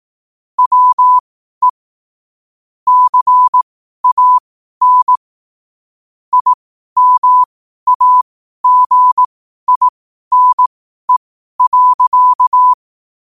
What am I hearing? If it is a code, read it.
Morse code, heard as a keyed tone, about 18 wpm, spaced as for 12 wpm, 1000 Hz. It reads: WE CAN IMAGINE.
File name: qotdfiles / 12wpm